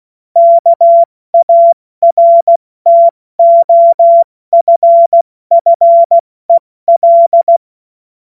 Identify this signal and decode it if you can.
Morse code, heard as a keyed tone, about 16 wpm, spaced as for 14 wpm, 680 Hz. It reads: KARTOFFEL
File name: Witz-16-September-2025-70-F80